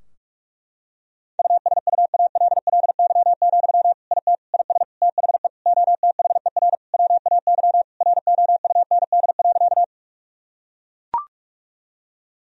Morse code, heard as a keyed tone, about 41 wpm, 700 Hz. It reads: USUALLY, IT IS THE OTHER WAY ROUND.
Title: Sentences from Top 1000 Words - Review Part 3 - 45wpm